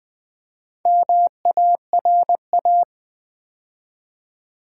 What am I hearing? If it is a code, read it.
Morse code, heard as a keyed tone, about 20 wpm, 700 Hz. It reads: MARA
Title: Build A Word - 2 Letters At A Time - 8-Letter Words - 20wpm